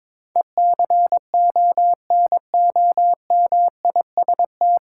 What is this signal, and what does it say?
Morse code, heard as a keyed tone, about 22 wpm, 700 Hz. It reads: ECONOMIST
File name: 9 Letter Words - First 3 Letters Given - 22wpm